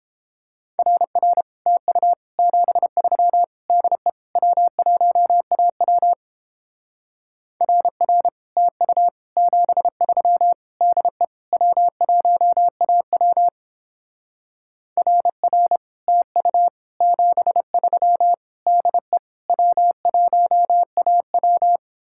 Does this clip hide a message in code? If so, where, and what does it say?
Yes: Morse code from 0.8 to 21.8 s, heard as a keyed tone, about 29 wpm, 700 Hz. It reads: RR TU 73 DE W1AW RR TU 73 DE W1AW RR TU 73 DE W1AW